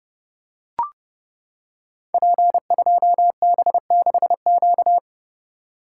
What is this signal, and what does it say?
Morse code, heard as a keyed tone, about 28 wpm, 700 Hz. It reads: P2B6Q
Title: Code Groups - 5 Letter:Numbers - 30wpm